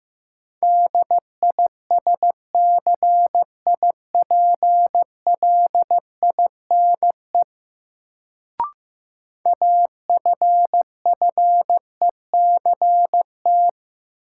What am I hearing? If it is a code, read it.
Morse code, heard as a keyed tone, about 15 wpm, 700 Hz. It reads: DISCIPLINE AFFECT
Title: Sentences from Top 900 Words - New Word Drill - 15wpm